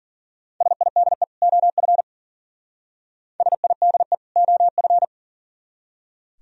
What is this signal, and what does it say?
Morse code, heard as a keyed tone, about 50 wpm, 700 Hz. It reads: SIDE OF SIDE OF
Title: Speed-Racing - Sets of 2 Words 40wpm